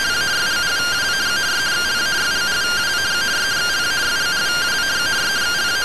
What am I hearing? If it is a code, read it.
Two-tone teleprinter signal, about 170 Hz shift.